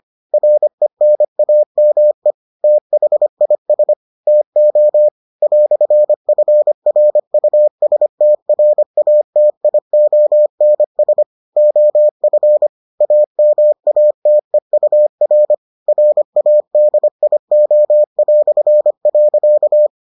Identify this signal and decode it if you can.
Morse code, heard as a keyed tone, about 25 wpm, 600 Hz. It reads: RENAME THIS TO "FRUSTRATIONS OF AMATEUR RADIO".